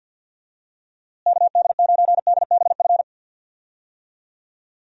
Morse code, heard as a keyed tone, about 47 wpm, 700 Hz. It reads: KD9DBF
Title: USA Call Signs - 6 Characters - Repeated 2x - 50wpm